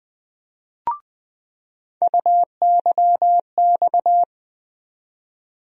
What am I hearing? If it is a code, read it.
Morse code, heard as a keyed tone, about 20 wpm, 700 Hz. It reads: UYX
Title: USA Call Sign Postfixes - 3 Letters 20wpm